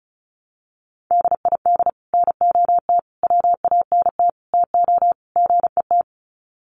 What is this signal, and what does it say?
Morse code, heard as a keyed tone, about 34 wpm, 700 Hz. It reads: DID NOT WANT TO GET
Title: Sets of 5 Words Encore2 35wpm